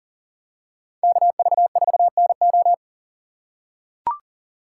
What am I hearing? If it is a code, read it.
Morse code, heard as a keyed tone, about 40 wpm, 700 Hz. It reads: KV4NO
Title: USA Call Signs - 5 Characters - 4x Character Spacing - 40wpm